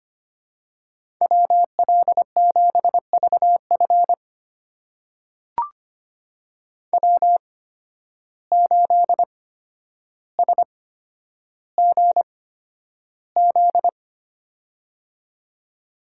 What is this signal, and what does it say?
Morse code, heard as a keyed tone, about 25 wpm, spaced as for 11 wpm, 700 Hz. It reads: WL7VF W8SGZ
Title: USA Call Signs - 5 Characters - 8x Character Spacing - 25wpm